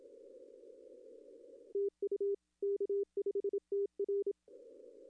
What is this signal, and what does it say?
Morse code, heard as a keyed tone, about 27 wpm, 400 Hz. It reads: TU K5TR
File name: contests / arrl-160-k5tr-op-w5tn -1300